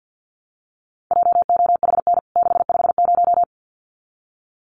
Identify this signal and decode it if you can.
Morse code, heard as a keyed tone, about 50 wpm, 700 Hz. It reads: JOHN 650
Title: CWT Exchange 50wpm